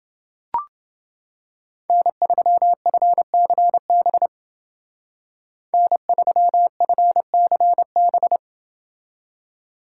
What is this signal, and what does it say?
Morse code, heard as a keyed tone, about 28 wpm, 700 Hz. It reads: N3FCB N3FCB